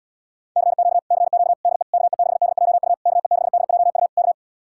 Morse code, heard as a keyed tone, about 79 wpm, 700 Hz. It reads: CQ CQ DE KE6DJU KE6DJU K